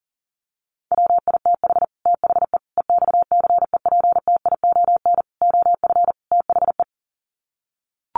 Morse code, heard as a keyed tone, about 40 wpm, 700 Hz. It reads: WITH THE EXCEPTION OF THE
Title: Sets of 5 Words Encore 40wpm